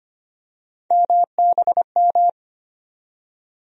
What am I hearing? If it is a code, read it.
Morse code, heard as a keyed tone, about 25 wpm, 700 Hz. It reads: MBM
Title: USA Call Sign Postfixes - 3 Letters 25wpm